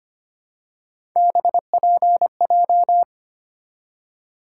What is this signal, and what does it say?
Morse code, heard as a keyed tone, about 25 wpm, 700 Hz. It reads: BPJ
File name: Three Letter-Number Combinations - 25wpm